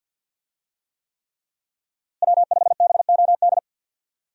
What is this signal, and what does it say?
Morse code, heard as a keyed tone, about 50 wpm, 700 Hz. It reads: W5BOD
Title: USA Call Signs - 5 Characters - Repeated 3x - 50wpm